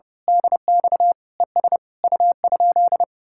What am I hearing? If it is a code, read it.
Morse code, heard as a keyed tone, about 28 wpm, 700 Hz. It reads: DX ES U?
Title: Speed-Racing Common QSO Phrases - 30wpm